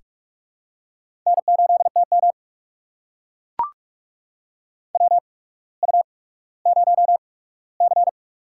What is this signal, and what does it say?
Morse code, heard as a keyed tone, about 44 wpm, spaced as for 19 wpm, 700 Hz. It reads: N8TM WU0C
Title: USA Call Signs - 4 Characters - 8x Character Spacing - 45wpm